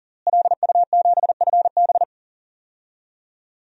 Morse code, heard as a keyed tone, about 40 wpm, 700 Hz. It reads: LU7FB
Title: DXCC-100 Call Signs - 5 Characters - Repeated 3x - 40wpm